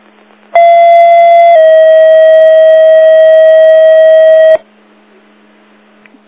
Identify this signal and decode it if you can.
Two-tone page, heard as a one-second tone, then a three-second tone, about 680 then 640 Hz. It is heard from 0.6 to 4.6 s.
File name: Big Moose 19 23 37